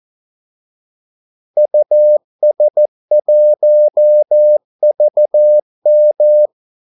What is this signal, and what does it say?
Morse code, heard as a keyed tone, about 14 wpm, 600 Hz. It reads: US1VM
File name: Callsigns-14wpm